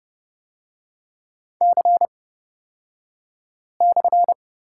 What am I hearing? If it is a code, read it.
Morse code, heard as a keyed tone, about 30 wpm, spaced as for 5 wpm, 700 Hz. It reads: C/
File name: Lesson 36e - BT - Three Characters 30wpm